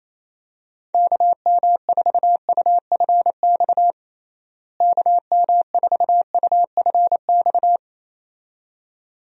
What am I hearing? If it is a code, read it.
Morse code, heard as a keyed tone, about 28 wpm, 700 Hz. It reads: KM4UFX KM4UFX
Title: USA Call Signs - 6 Characters - Repeated 2x - 28wpm